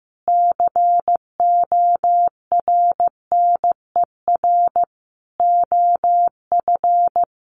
Morse code, heard as a keyed tone, about 15 wpm, 700 Hz. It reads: CORNER OF
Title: Sets of 5 Words Encore 15wpm